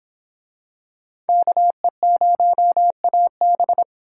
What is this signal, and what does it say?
Morse code, heard as a keyed tone, about 26 wpm, 700 Hz. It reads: KE0AB